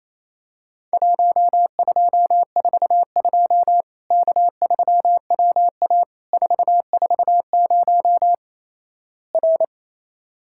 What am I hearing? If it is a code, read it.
Morse code, heard as a keyed tone, about 28 wpm, 700 Hz. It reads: 1242 K3WA 440 R
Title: Worked All Europe DX Contest - QTC Exchange - 28wpm